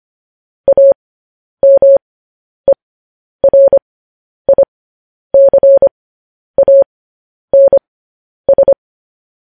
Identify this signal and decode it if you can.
Morse code, heard as a keyed tone, about 25 wpm, spaced as for 10 wpm, 550 Hz. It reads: AMERICANS
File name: podcast / News headlines-1x-25x10